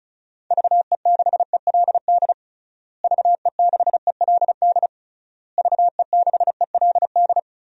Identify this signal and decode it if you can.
Morse code, heard as a keyed tone, about 36 wpm, 700 Hz. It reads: VE6ELD VE6ELD VE6ELD
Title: DXCC-100 Call Signs - 6 Characters - Repeated 3x - 35wpm